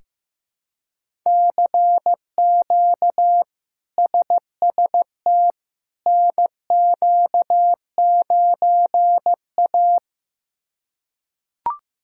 Morse code, heard as a keyed tone, about 15 wpm, 700 Hz. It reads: CQ SST NQ9A
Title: K1USN SST QSO Practice - 15wpm